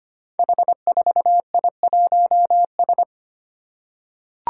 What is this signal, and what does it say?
Morse code, heard as a keyed tone, about 25 wpm, 700 Hz. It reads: H4I1S
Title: Five Challenging Letter-Number Combinations 25wpm